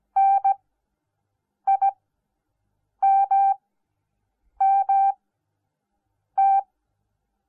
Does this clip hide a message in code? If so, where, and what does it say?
Yes: Morse code from 0.2 to 6.6 s, heard as a keyed tone, about 17 wpm, spaced as for 6 wpm, 780 Hz. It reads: NIMMT